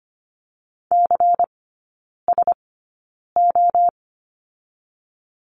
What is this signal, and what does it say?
Morse code, heard as a keyed tone, about 25 wpm, spaced as for 9 wpm, 700 Hz. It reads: CSO
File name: course / 25:10wpm / 093 Lesson 16e - Letter c - Three Characters 25wpm